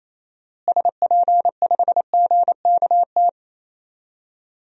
Morse code, heard as a keyed tone, about 28 wpm, 700 Hz. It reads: SP5GKT
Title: DXCC-100 Call Signs - 6 Characters - Repeated 3x - 28wpm